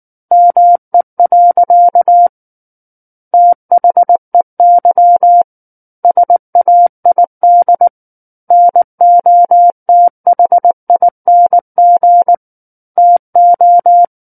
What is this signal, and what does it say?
Morse code, heard as a keyed tone, about 19 wpm, 700 Hz. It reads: ME. THEY SAID NOTHING TO
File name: Book One / 10-Chapter 10 - 19 WPM